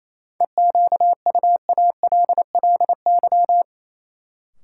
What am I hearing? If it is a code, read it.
Morse code, heard as a keyed tone, about 28 wpm, 700 Hz. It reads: EQUALLY
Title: Sentences from Top 1400 Words - New Word Drill Alt - 28wpm